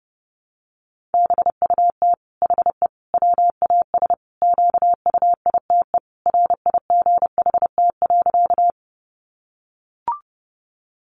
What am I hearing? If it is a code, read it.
Morse code, heard as a keyed tone, about 28 wpm, 700 Hz. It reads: BUT HE WAS QUITE RIGHT.
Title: Sentences from Top 700 Words 30wpm